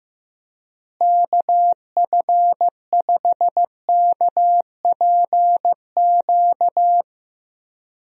Morse code, heard as a keyed tone, about 15 wpm, 700 Hz. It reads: KF5KPQ
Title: USA Call Signs - 6 Characters - Repeated 2x - 15wpm